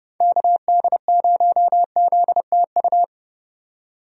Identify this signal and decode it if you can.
Morse code, heard as a keyed tone, about 31 wpm, 700 Hz. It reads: KD0ZTU
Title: USA Call Signs - 6 Characters - 30wpm